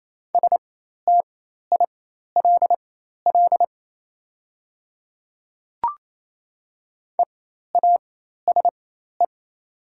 Morse code, heard as a keyed tone, about 28 wpm, spaced as for 13 wpm, 700 Hz. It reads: STILL EASE
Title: 4x Character Spacing - Top 300 Words - 28wpm